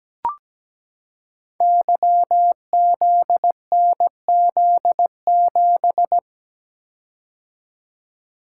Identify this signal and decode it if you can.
Morse code, heard as a keyed tone, about 17 wpm, 700 Hz. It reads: YZNZ7
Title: Code Groups - 5 Letter:Numbers - 17wpm